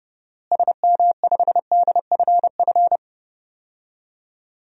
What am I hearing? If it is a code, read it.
Morse code, heard as a keyed tone, about 30 wpm, 700 Hz. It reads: SM5DFF
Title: DXCC-100 Call Signs - 6 Characters - Repeated 2x - 30wpm